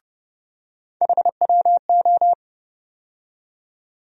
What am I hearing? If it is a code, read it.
Morse code, heard as a keyed tone, about 30 wpm, 700 Hz. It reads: HWO
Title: USA Call Sign Postfixes - 3 Letters 30wpm